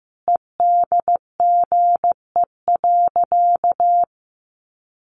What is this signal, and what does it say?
Morse code, heard as a keyed tone, about 14 wpm, 700 Hz. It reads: EDGE.
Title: Additional Easy Phrases 15wpm